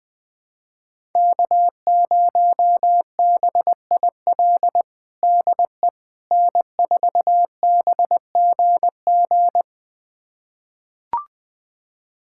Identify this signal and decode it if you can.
Morse code, heard as a keyed tone, about 21 wpm, 700 Hz. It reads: K0BIL DE N4BGG
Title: USA Callsign DE Callsign - 5 Characters - 20wpm